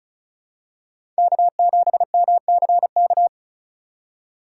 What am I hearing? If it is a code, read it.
Morse code, heard as a keyed tone, about 36 wpm, 700 Hz. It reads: K7MCK
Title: USA Call Signs - 3x Character Spacing - 35wpm